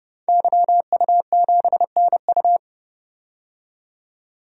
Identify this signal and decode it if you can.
Morse code, heard as a keyed tone, about 30 wpm, 700 Hz. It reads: YU7NU
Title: DXCC-100 Call Signs - 5 Characters - 30wpm